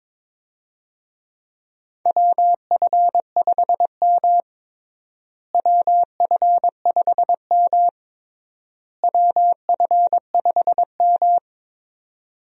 Morse code, heard as a keyed tone, about 22 wpm, 700 Hz. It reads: WF5M WF5M WF5M
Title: USA Call Signs - 4 Characters - Repeated 3x - 22wpm